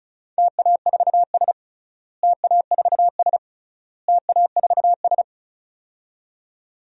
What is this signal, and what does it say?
Morse code, heard as a keyed tone, about 34 wpm, 700 Hz. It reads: TA4S TA4S TA4S
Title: DXCC-100 Call Signs - 4 Characters - Repeated 3x - 35wpm